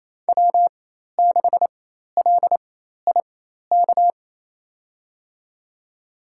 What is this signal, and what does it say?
Morse code, heard as a keyed tone, about 28 wpm, spaced as for 13 wpm, 700 Hz. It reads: W6LIK
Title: USA Call Signs - 5 Characters - 4x Character Spacing - 28wpm